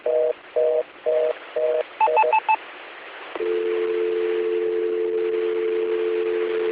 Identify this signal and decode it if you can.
Reorder tone (North America), 480 + 620 Hz, with noise.
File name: recordings / AC 2017 Anthrocon / qsorec SimplexLogic EDT